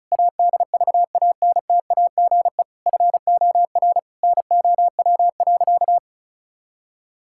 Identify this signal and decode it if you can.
Morse code, heard as a keyed tone, about 36 wpm, 700 Hz. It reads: ADVANTAGE FOR NOW.
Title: Sentences from Top 1000 Words - Review Part 6 - 35wpm